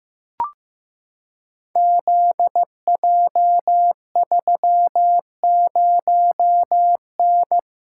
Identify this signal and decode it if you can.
Morse code, heard as a keyed tone, about 15 wpm, 700 Hz. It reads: ZJ30N
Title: Code Groups - 5 Letter:Numbers - 15wpm